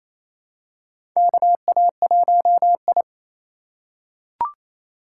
Morse code, heard as a keyed tone, about 28 wpm, 700 Hz. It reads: KA1I
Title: USA Call Signs - 4 Characters - 2x Character Spacing - 28wpm